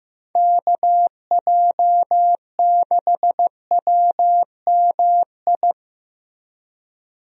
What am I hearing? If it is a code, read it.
Morse code, heard as a keyed tone, about 15 wpm, 700 Hz. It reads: KJ6WMI
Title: USA Call Signs - 6 Characters - Repeated 3x - 15wpm